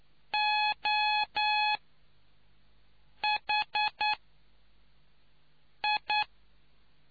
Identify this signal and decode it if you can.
Morse code, heard as a keyed tone, about 9 wpm, spaced as for 4 wpm, 840 Hz. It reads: OHI